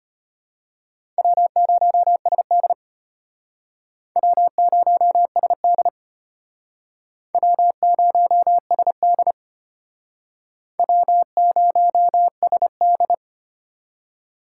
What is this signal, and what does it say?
Morse code, heard as a keyed tone, about 32 wpm, 700 Hz. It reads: W0SD W0SD W0SD W0SD